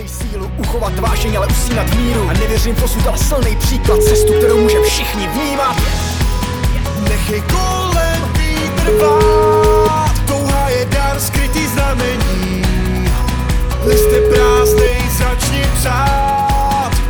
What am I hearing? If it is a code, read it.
Ringback tone (Europe, 425 Hz), with speech.